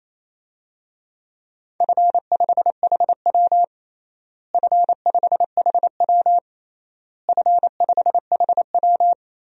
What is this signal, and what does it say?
Morse code, heard as a keyed tone, about 28 wpm, 700 Hz. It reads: F5HW F5HW F5HW
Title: DXCC-100 Call Signs - 4 Characters - Repeated 3x - 28wpm